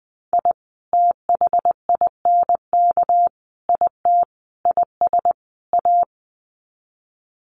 Morse code, heard as a keyed tone, about 20 wpm, 700 Hz. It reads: I THINK IT IS A
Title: Sets of 5 Words Encore 20wpm